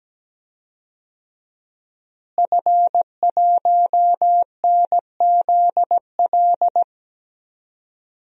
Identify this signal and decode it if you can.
Morse code, heard as a keyed tone, about 17 wpm, 700 Hz. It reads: F1NZL
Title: DXCC-100 Call Signs - 5 Characters - Repeated 3x - 17wpm